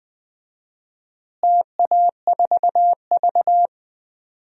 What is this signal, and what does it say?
Morse code, heard as a keyed tone, about 21 wpm, 700 Hz. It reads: TA4V
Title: DXCC-100 Call Signs - 4 Characters - Repeated 3x - 20wpm